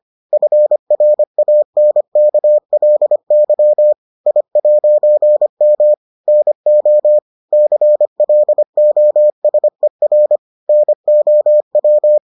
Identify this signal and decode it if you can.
Morse code, heard as a keyed tone, about 25 wpm, 600 Hz. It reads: FRANKLY I'M NO CLOSER NOW